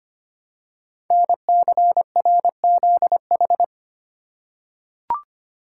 Morse code, heard as a keyed tone, about 25 wpm, 700 Hz. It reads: NCRZH
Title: Code Groups - 5 Letters - 25wpm